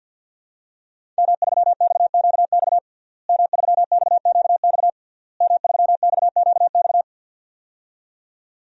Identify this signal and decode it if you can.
Morse code, heard as a keyed tone, about 50 wpm, 700 Hz. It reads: M3XQX M3XQX M3XQX
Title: DXCC-100 Call Signs - 5 Characters - Repeated 3x - 50wpm